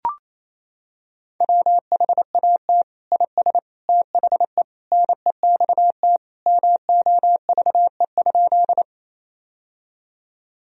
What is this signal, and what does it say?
Morse code, heard as a keyed tone, about 28 wpm, 700 Hz. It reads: WHAT IS THE NEXT MOVE?
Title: Sentences from Top 1000 Words - Review Part 1 - 28wpm